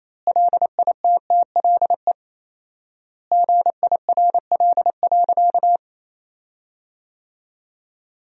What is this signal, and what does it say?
Morse code, heard as a keyed tone, about 27 wpm, 700 Hz. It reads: LITTLE GIRL.
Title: 4x Word-Spacing - Sentences from Top 400 Words - 28wpm